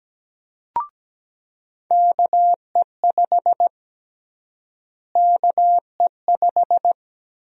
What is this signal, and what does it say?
Morse code, heard as a keyed tone, about 17 wpm, 700 Hz. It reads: KE5 KE5
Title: USA Call Sign Prefixes - Repeated 2x - 17wpm